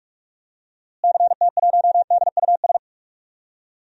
Morse code, heard as a keyed tone, about 46 wpm, 700 Hz. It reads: CT1DUS
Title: DXCC-100 Call Signs - 6 Characters - Repeated 2x - 45wpm